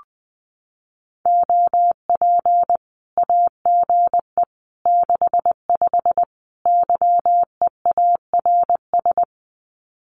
Morse code, heard as a keyed tone, about 20 wpm, 700 Hz. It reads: OP AGE 65 YEARS
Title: Common QSO Phrases 20wpm